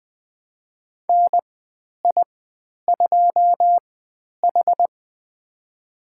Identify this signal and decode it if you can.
Morse code, heard as a keyed tone, about 20 wpm, spaced as for 10 wpm, 700 Hz. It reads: NI2H
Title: 095 Lesson 16g - Letter c - Call Signs 20wpm